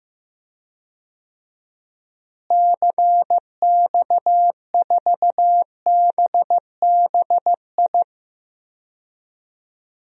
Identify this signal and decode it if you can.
Morse code, heard as a keyed tone, about 15 wpm, 700 Hz. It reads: CX4BBI
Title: DXCC-100 Call Signs - 6 Characters - Repeated 2x - 15wpm